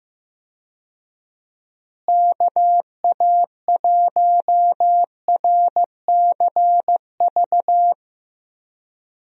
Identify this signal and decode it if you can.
Morse code, heard as a keyed tone, about 15 wpm, 700 Hz. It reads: KA1RCV
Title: USA Call Signs - 6 Characters - Repeated 2x - 15wpm